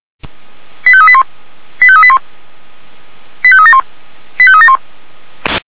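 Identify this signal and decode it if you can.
Five-tone selective call (ZVEI), heard as quick runs of five tones, tones of about 70 ms. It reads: 85381 85381 85371 85371
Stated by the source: SDR-received ZVEI five-tone selective call.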